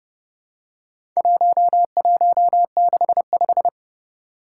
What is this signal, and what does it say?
Morse code, heard as a keyed tone, about 29 wpm, 700 Hz. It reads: 1165
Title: Four Challenging Letter-Number Combinations 30wpm